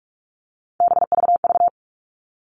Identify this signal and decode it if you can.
Morse code, heard as a keyed tone, about 44 wpm, 700 Hz. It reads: BVV
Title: All Cut Numbers - 3-Digit 45wpm